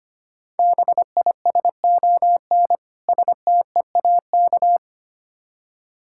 Morse code, heard as a keyed tone, about 25 wpm, 700 Hz. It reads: BISON STEAK
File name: practice / 25wpm / Difficult Words with Hints - 25wpm